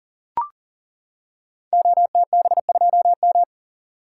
Morse code, heard as a keyed tone, about 40 wpm, 690 Hz. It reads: OTB2M
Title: Code Groups - 5 Letter:Numbers - 40wpm